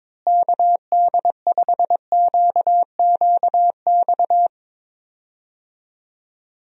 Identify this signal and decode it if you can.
Morse code, heard as a keyed tone, about 22 wpm, 700 Hz. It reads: KD5QQX
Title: USA Call Signs - 6 Characters - Repeated 2x - 22wpm